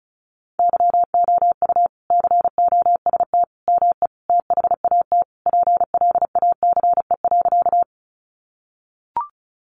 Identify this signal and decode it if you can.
Morse code, heard as a keyed tone, about 34 wpm, 700 Hz. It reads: YOU COST ME THAT PLACE.
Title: Sentences from Top 800 Words 35wpm